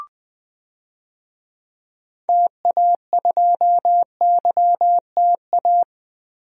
Morse code, heard as a keyed tone, about 20 wpm, 700 Hz. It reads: TA2YTA
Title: DXCC-100 Call Signs - 6 Characters - 20wpm